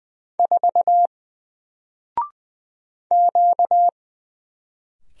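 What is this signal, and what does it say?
Morse code, heard as a keyed tone, about 20 wpm, 700 Hz. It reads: 4 Q
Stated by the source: Single letter and number Morse drill at 20 wpm, from N4LCD.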